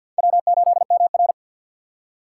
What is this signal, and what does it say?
Morse code, heard as a keyed tone, about 52 wpm, 700 Hz. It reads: W8MR
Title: USA Call Signs - 4 Characters - 2x Character Spacing - 50wpm